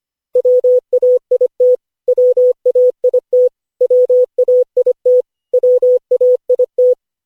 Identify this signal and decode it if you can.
Morse code, heard as a keyed tone, about 25 wpm, 500 Hz. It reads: WAIT WAIT WAIT WAIT
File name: cw / Session5 cw training